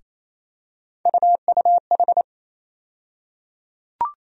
Morse code, heard as a keyed tone, about 27 wpm, 700 Hz. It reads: UUH
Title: USA Call Sign Postfixes - 3 Letters 28wpm